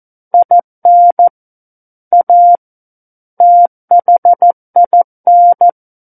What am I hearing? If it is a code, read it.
Morse code, heard as a keyed tone, about 14 wpm, 700 Hz. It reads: IN A THIN
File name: Book One / 05-Chapter 5 - 14 WPM